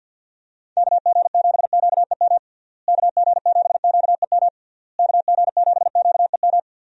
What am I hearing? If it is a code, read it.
Morse code, heard as a keyed tone, about 53 wpm, 700 Hz. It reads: KG7QEM KG7QEM KG7QEM
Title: USA Call Signs - 6 Characters - Repeated 3x - 50wpm